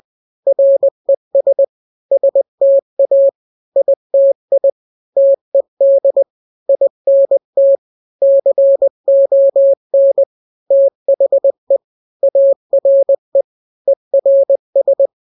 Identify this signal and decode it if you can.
Morse code, heard as a keyed tone, about 20 wpm, 550 Hz. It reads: RES STA ITI TED INT CON THE ARE ERS
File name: n-grams / 3-grams 20 18 WPM